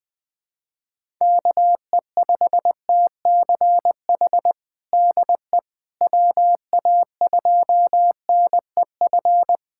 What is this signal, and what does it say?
Morse code, heard as a keyed tone, about 20 wpm, 700 Hz. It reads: KE5TCH DE WA2NEF
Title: USA Callsign DE Callsign 20wpm